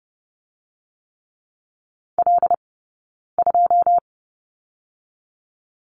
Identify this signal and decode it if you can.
Morse code, heard as a keyed tone, about 30 wpm, 700 Hz. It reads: L 2
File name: Lesson 20d - Number 3 - Two Characters 30wpm